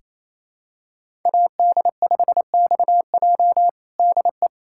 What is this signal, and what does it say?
Morse code, heard as a keyed tone, about 28 wpm, 700 Hz. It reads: AD5XJ DE
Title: USA Callsign DE Callsign 28wpm